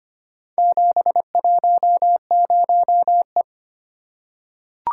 Morse code, heard as a keyed tone, about 25 wpm, 700 Hz. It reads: 710E
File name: practice / 25wpm / Four Letter-Number Combinations - 25wpm